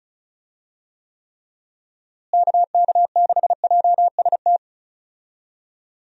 Morse code, heard as a keyed tone, about 34 wpm, 700 Hz. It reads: KK6JST
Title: USA Call Signs - 6 Characters - Repeated 2x - 35wpm